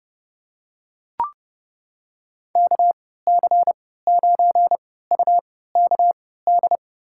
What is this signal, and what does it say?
Morse code, heard as a keyed tone, about 30 wpm, spaced as for 17 wpm, 700 Hz. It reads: KC9UKD
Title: USA Call Signs - 3x Character Spacing - 30wpm